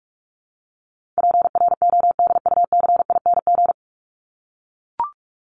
Morse code, heard as a keyed tone, about 44 wpm, 700 Hz. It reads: PRODUCING
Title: ING Suffix - Top 100 Words with Base Words 45wpm